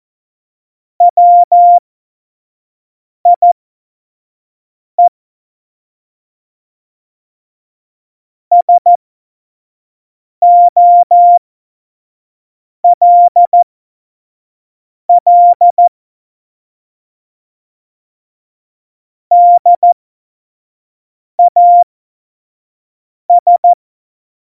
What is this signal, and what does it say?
Morse code, heard as a keyed tone, about 14 wpm, spaced as for 5 wpm, 700 Hz. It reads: WIE SOLL DAS
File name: Morsekurs / Witze / Witz 06. Oktober 2020, 25 F70